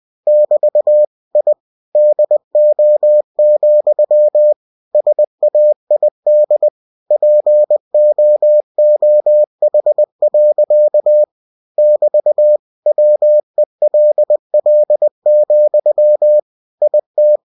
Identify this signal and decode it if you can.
Morse code, heard as a keyed tone, about 20 wpm, 600 Hz. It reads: <BT> I DO, SAID POOH. <BT> WELL, IT
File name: cw-books / winnie-the-pooh / winnie-20wpm-page-0005